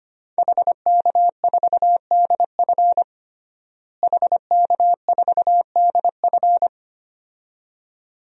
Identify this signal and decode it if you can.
Morse code, heard as a keyed tone, about 25 wpm, 700 Hz. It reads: HK4DF HK4DF
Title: DXCC-100 Call Signs - 5 Characters - Repeated 2x - 25wpm